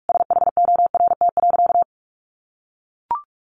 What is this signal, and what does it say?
Morse code, heard as a keyed tone, about 44 wpm, 700 Hz. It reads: SHORT.
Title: Sentences from Top 500 Words 45wpm